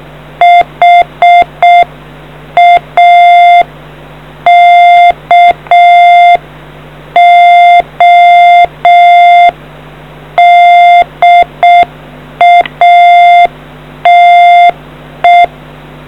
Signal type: Morse code